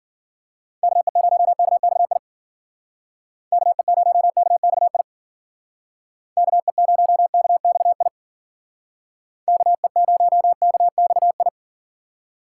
Morse code, heard as a keyed tone, about 53 wpm, 700 Hz. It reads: KE0KXI KE0KXI KE0KXI KE0KXI